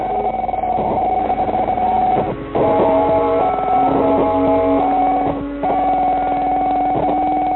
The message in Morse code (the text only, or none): none